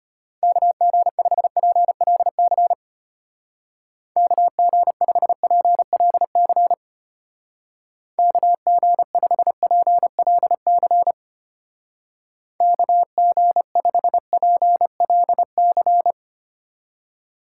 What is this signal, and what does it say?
Morse code, heard as a keyed tone, about 33 wpm, 700 Hz. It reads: KG5PLC KG5PLC KG5PLC KG5PLC